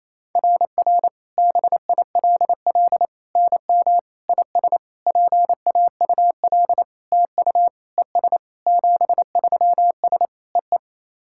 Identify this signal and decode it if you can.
Morse code, heard as a keyed tone, about 28 wpm, 690 Hz. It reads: RR BILL NM IS PAUL TU ES 73S E E
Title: SOTA QSO Practice 28wpm